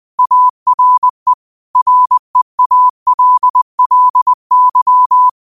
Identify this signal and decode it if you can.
Morse code, heard as a keyed tone, about 20 wpm, 1000 Hz. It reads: ARE REALLY
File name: qotdfiles / 20wpm